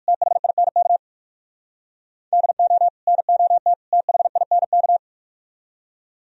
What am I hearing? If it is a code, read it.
Morse code, heard as a keyed tone, about 45 wpm, 700 Hz. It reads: THINK DO NOT THINK
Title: Speed-Racing - Sets of 3 Words 45wpm